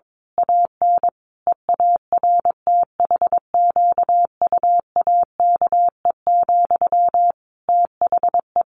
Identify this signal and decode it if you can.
Morse code, heard as a keyed tone, about 22 wpm, 700 Hz. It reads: AN EARTHQUAKE, THE
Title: A Sound Of Thunder 22wpm